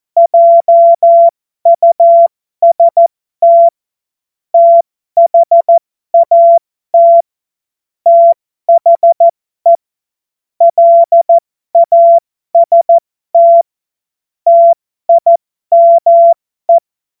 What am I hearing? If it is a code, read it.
Morse code, heard as a keyed tone, about 14 wpm, spaced as for 12 wpm, 680 Hz. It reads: JUST THAT THE LAST TIME